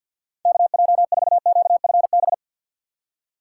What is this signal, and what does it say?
Morse code, heard as a keyed tone, about 53 wpm, 690 Hz. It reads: KJ4QFB
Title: USA Call Signs - 6x Character Spacing - 50wpm